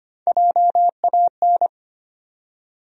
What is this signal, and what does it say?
Morse code, heard as a keyed tone, about 25 wpm, 700 Hz. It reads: JAN
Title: USA Common Names - Max 3 Letters - 25wpm